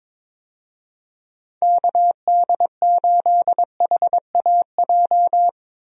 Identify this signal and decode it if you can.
Morse code, heard as a keyed tone, about 22 wpm, 700 Hz. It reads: KD8HAJ
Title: USA Call Signs - 6 Characters - Repeated 3x - 22wpm